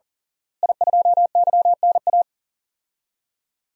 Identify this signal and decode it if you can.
Morse code, heard as a keyed tone, about 37 wpm, 700 Hz. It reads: I2YNA
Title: DXCC-100 Call Signs - 5 Characters - Repeated 2x - 40wpm